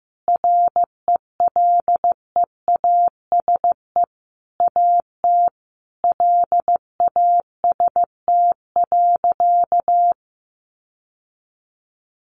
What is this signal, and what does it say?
Morse code, heard as a keyed tone, about 15 wpm, 700 Hz. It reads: RELEASE AT LAST.
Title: Sentences from Top 1100 Words 15wpm